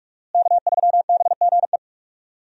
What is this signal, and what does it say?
Morse code, heard as a keyed tone, about 42 wpm, 700 Hz. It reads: K3BGE
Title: USA Call Signs - 5 Characters - Repeated 2x - 45wpm